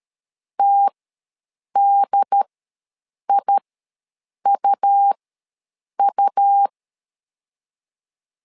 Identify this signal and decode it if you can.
Morse code, heard as a keyed tone, about 13 wpm, spaced as for 7 wpm, 800 Hz. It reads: TDIUU
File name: LEZIONI / 2 UHDN / 2 UHD ws 12wpm 1722